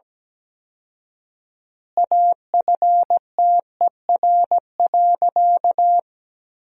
Morse code, heard as a keyed tone, about 17 wpm, 700 Hz. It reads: AFTER.
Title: 4x Word-Spacing - Sentences from Top 500 Words - 17wpm